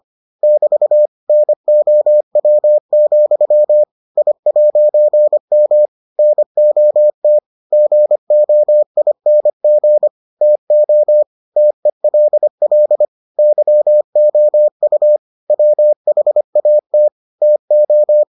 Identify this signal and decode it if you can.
Morse code, heard as a keyed tone, about 25 wpm, 600 Hz. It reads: <BT> NOW, I'M NOT GOING TO TELL YOU WHAT TO